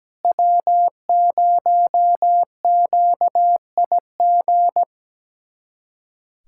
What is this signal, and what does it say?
Morse code, heard as a keyed tone, about 17 wpm, 700 Hz. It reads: W0QIG
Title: USA Callsign DE Callsign - 5 Characters - 17wpm